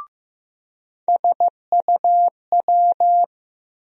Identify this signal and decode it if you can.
Morse code, heard as a keyed tone, about 15 wpm, 700 Hz. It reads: SUW
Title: USA Call Sign Postfixes - 3 Letters 15wpm